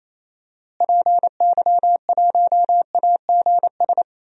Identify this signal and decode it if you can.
Morse code, heard as a keyed tone, about 28 wpm, 700 Hz. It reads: PY1AGS
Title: DXCC-100 Call Signs - 6 Characters - Repeated 2x - 28wpm